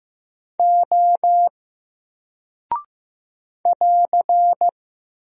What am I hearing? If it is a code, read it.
Morse code, heard as a keyed tone, about 15 wpm, 700 Hz. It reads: O <AR>
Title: Single Character (Letter:Number:Punctuation:Prosign) 15wpm